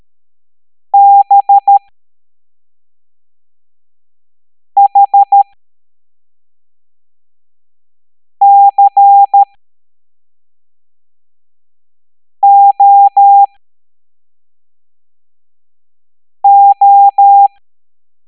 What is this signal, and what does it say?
Morse code, heard as a keyed tone, about 13 wpm, spaced as for 3 wpm, 800 Hz. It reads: BHCOO
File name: hamdata / L07-EX03-HQP